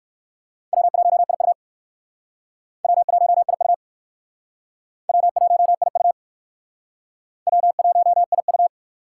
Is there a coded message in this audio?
Yes: Morse code from 0.7 to 8.7 s, heard as a keyed tone, about 61 wpm, 700 Hz. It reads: W1IU W1IU W1IU W1IU